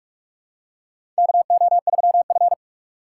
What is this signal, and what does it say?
Morse code, heard as a keyed tone, about 46 wpm, 700 Hz. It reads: KO3F
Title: USA Call Signs - 4 Characters - 4x Character Spacing - 45wpm